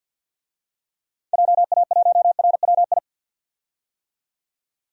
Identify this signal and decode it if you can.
Morse code, heard as a keyed tone, about 45 wpm, 700 Hz. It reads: JA1RWI
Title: DXCC-100 Call Signs - 6 Characters - 50wpm